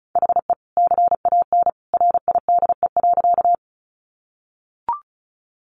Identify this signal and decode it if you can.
Morse code, heard as a keyed tone, about 34 wpm, 700 Hz. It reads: HE CAN RIDE.
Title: Sentences from Top 1200 Words 35wpm